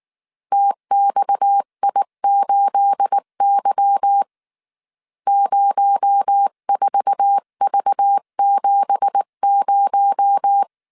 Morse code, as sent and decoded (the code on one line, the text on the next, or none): - -..- .. ---.. -.-- / ----- ....- ...- --... -----
TXI8Y 04V70